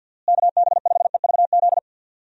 Morse code, heard as a keyed tone, about 48 wpm, 700 Hz. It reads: KB5EVZ
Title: USA Call Signs - 6 Characters - Repeated 2x - 50wpm